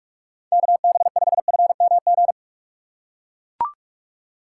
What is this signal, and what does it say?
Morse code, heard as a keyed tone, about 41 wpm, 700 Hz. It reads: KB5FMG
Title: USA Call Signs - 6 Characters - 4x Character Spacing - 45wpm